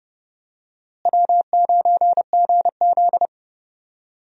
Morse code, heard as a keyed tone, about 30 wpm, 700 Hz. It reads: W9GZ
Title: USA Call Signs - 4 Characters - Repeated 3x - 30wpm